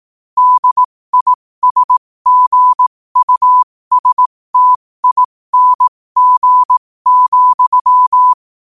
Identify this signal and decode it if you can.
Morse code, heard as a keyed tone, about 18 wpm, spaced as for 15 wpm, 1000 Hz. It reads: DISGUSTING,